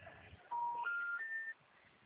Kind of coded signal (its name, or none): telephone call progress tone